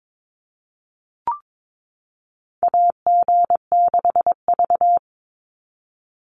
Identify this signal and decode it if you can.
Morse code, heard as a keyed tone, about 22 wpm, 700 Hz. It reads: AG6V
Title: NA SOTA Chaser Callsigns 22wpm